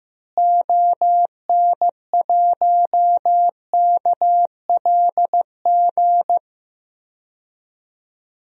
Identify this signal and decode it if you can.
Morse code, heard as a keyed tone, about 15 wpm, 700 Hz. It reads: ON1KLG
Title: DXCC-100 Call Signs - 6 Characters - Repeated 3x - 15wpm